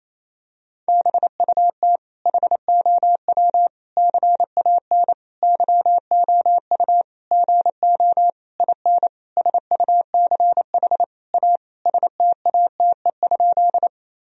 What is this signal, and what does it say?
Morse code, heard as a keyed tone, about 28 wpm, 700 Hz. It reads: BUT HOW CAN YOU GO IN SUCH A STATE?
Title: Sentences from Top 1000 Words - Review Part 2 - 28wpm